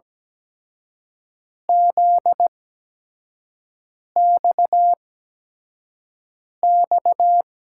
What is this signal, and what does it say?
Morse code, heard as a keyed tone, about 17 wpm, spaced as for 5 wpm, 700 Hz. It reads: ZXX
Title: USA Call Signs - 6 Characters - 8x Character Spacing - 17wpm